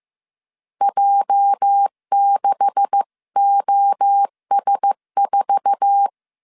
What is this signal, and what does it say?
Morse code, heard as a keyed tone, about 15 wpm, 800 Hz. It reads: J6OS4
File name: CW / DET / 1 9 1500C S A 14wpm 3222